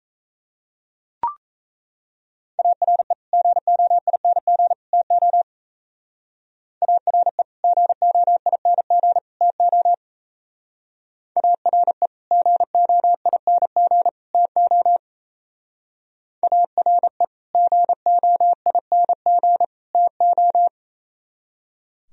Morse code, heard as a keyed tone, about 33 wpm, 700 Hz. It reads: ARE GOING TO ARE GOING TO ARE GOING TO ARE GOING TO
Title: Speed-Racing - Sets of 3 Words 28wpm